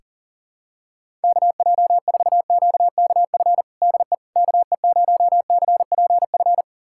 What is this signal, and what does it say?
Morse code, heard as a keyed tone, about 40 wpm, 700 Hz. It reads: KJ4QKF DE KE0CPF
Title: USA Callsign DE Callsign - 6 Characters - 40wpm